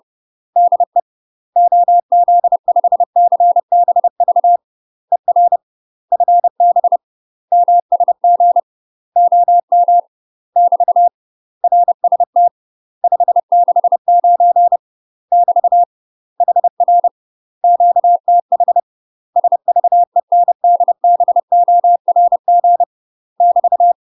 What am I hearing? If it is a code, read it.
Morse code, heard as a keyed tone, about 30 wpm, 700 Hz. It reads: DE OZ5CBV ER FB MSG OM <BT> RST 569 <BT> HR QTH SVENDBORG <BT>
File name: UK MASTER QSOs TRAINING / 30WPM / UK MASTER QSOs 0039